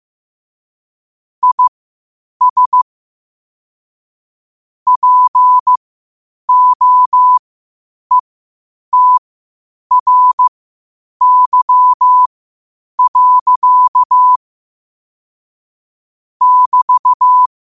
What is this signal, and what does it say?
Morse code, heard as a keyed tone, about 15 wpm, spaced as for 8 wpm, 1000 Hz. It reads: IS POETRY. <BT>
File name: qotdfiles / 8wpm